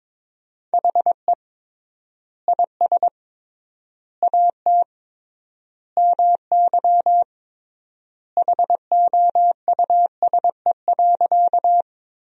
Morse code, heard as a keyed tone, about 22 wpm, 700 Hz. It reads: HE IS AT MY HOUSE.
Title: 3x Word-Spacing - Sentences from Top 200 Words - 22wpm